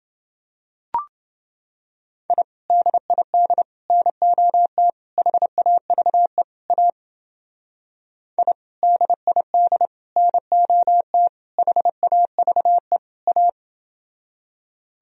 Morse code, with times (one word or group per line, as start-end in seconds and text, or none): 2.3-2.4 I
2.7-3.6 DID
3.9-4.9 NOT
5.2-6.4 HAVE
6.7-6.9 A
8.4-8.5 I
8.8-9.9 DID
10.2-11.3 NOT
11.6-13.0 HAVE
13.3-13.5 A